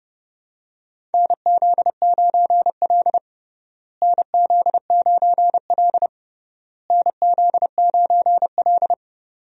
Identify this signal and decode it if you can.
Morse code, heard as a keyed tone, about 30 wpm, 700 Hz. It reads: NZ9L NZ9L NZ9L
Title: USA Call Signs - 4 Characters - Repeated 3x - 30wpm